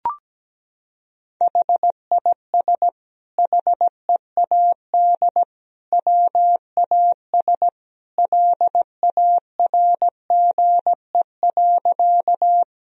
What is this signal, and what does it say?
Morse code, heard as a keyed tone, about 17 wpm, 700 Hz. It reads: HIS HEAD WAS LARGE.
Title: Sentences from Top 1000 Words - Review Part 1 - 17wpm